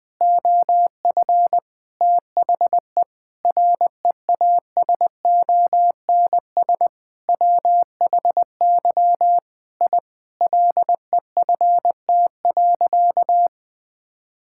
Morse code, heard as a keyed tone, about 20 wpm, 700 Hz. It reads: OF THE REASONS WHY I LEFT.
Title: Sentences with 5-Word Phrases - 20wpm